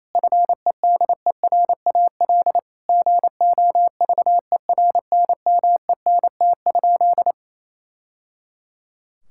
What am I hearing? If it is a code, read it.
Morse code, heard as a keyed tone, about 28 wpm, 700 Hz. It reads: FEDERAL GOVERNMENT?
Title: Sentences from Top 1000 Words - Review Part 5 - 28wpm